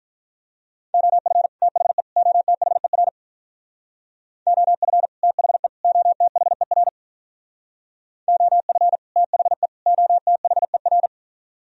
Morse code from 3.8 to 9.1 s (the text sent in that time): OF THE OTHER OF